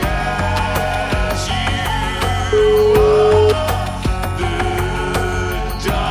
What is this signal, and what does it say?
Ringback tone (Europe, 425 Hz), with noise.